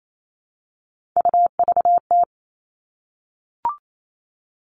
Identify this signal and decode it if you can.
Morse code, heard as a keyed tone, about 28 wpm, 700 Hz. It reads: UVT
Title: All Cut Numbers - 3-Digit 28wpm